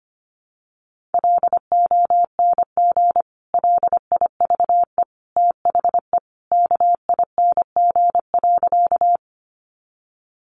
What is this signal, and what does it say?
Morse code, heard as a keyed tone, about 25 wpm, 700 Hz. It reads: LONG LIVE THE KING.
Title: Sentences from Top 400 Words 25wpm